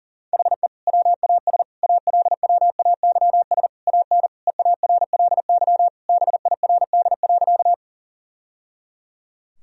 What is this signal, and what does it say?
Morse code, heard as a keyed tone, about 40 wpm, 700 Hz. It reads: HE WAS ALWAYS AN EARLY BIRD.
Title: Sentences from Top 1000 Words - Review Part 2 - 40wpm